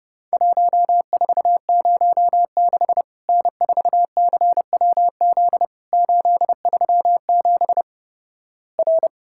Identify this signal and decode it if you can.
Morse code, heard as a keyed tone, about 30 wpm, 700 Hz. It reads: 1406 N4CWZ 837 R
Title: Worked All Europe DX Contest - QTC Exchange - 30wpm